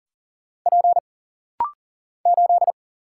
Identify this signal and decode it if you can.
Morse code, heard as a keyed tone, about 40 wpm, 700 Hz. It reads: P 8
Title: Single Letter-Number - Rapid-Fire 40wpm